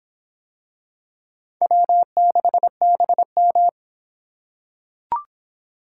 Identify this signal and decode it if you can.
Morse code, heard as a keyed tone, about 25 wpm, 700 Hz. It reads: W6BM